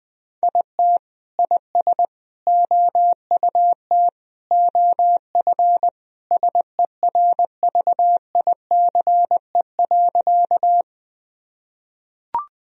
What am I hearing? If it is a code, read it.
Morse code, heard as a keyed tone, about 20 wpm, 700 Hz. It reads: IT IS OUT OF SERVICE.
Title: Sentences from Top 1000 Words - Review Part 4 - 20wpm